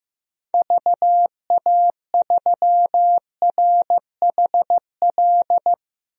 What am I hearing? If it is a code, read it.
Morse code, heard as a keyed tone, about 15 wpm, 700 Hz. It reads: VA3RHL
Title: DXCC-100 Call Signs - 6 Characters - 15wpm